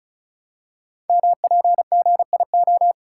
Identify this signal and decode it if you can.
Morse code, heard as a keyed tone, about 36 wpm, 690 Hz. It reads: MPGIO
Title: Code Groups - 5 Letters - 35wpm